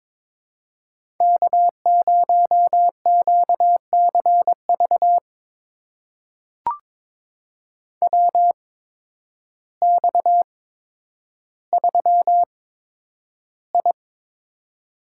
Morse code, heard as a keyed tone, about 22 wpm, spaced as for 10 wpm, 700 Hz. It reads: K0QCV WX3I
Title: USA Call Signs - 8x Character Spacing - 22wpm